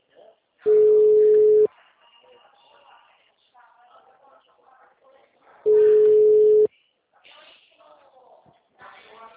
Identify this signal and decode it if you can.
Ringback tone (Europe, 425 Hz), with speech.